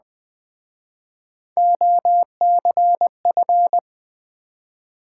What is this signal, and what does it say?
Morse code, heard as a keyed tone, about 21 wpm, 700 Hz. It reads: OCF
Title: Speed-Racing Top 180 QSO Elements 17wpm